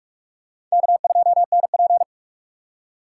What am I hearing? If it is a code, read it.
Morse code, heard as a keyed tone, about 43 wpm, 700 Hz. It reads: K2NP